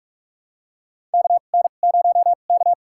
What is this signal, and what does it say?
Morse code, heard as a keyed tone, about 46 wpm, spaced as for 33 wpm, 700 Hz. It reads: KN0K
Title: USA Call Signs - 4 Characters - 2x Character Spacing - 45wpm